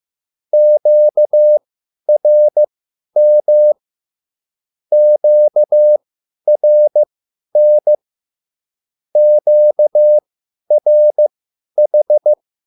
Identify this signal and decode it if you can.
Morse code, heard as a keyed tone, about 15 wpm, spaced as for 10 wpm, 600 Hz. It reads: QRM QRN QRH